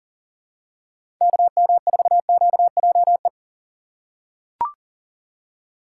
Morse code, heard as a keyed tone, about 39 wpm, 700 Hz. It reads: KM4QJE
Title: USA Call Signs - 6 Characters - 6x Character Spacing - 40wpm